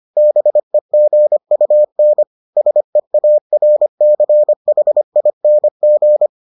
Morse code, heard as a keyed tone, about 25 wpm, 600 Hz. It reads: BEGUN SEARCHING